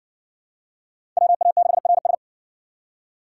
Morse code, heard as a keyed tone, about 57 wpm, 700 Hz. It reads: WA6RS